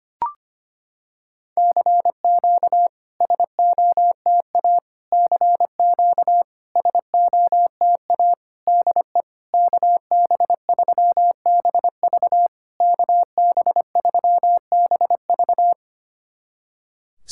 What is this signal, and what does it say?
Morse code, heard as a keyed tone, about 25 wpm, 690 Hz. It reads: CQ SOTA CQ SOTA DE KB3BV KB3BV
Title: SOTA QSO Practice 25wpm